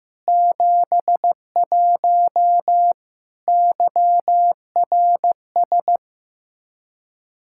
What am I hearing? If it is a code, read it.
Morse code, heard as a keyed tone, about 15 wpm, 700 Hz. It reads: 71 YRS
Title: Practice QSO - Exchange 3 - More Info - 15wpm